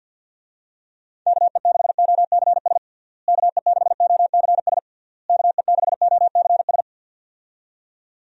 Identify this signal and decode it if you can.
Morse code, heard as a keyed tone, about 52 wpm, 700 Hz. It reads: KE6OCS KE6OCS KE6OCS
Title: USA Call Signs - 6 Characters - Repeated 3x - 50wpm